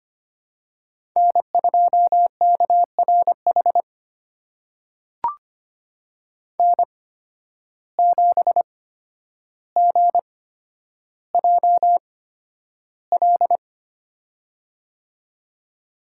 Morse code, heard as a keyed tone, about 25 wpm, spaced as for 11 wpm, 700 Hz. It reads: N2KRH N7GJL
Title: USA Call Signs - 5 Characters - 8x Character Spacing - 25wpm